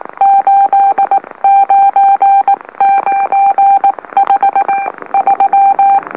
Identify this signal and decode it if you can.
Morse code, heard as a keyed tone, about 18 wpm, spaced as for 16 wpm, 790 Hz. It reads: 89953
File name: M04- -2345utc-LOLO